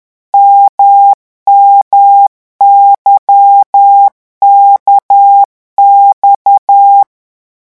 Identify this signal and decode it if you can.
Morse code, heard as a keyed tone, about 10 wpm, 800 Hz. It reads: MMYKX
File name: morse mp3 / revision2-vitesse 10 mots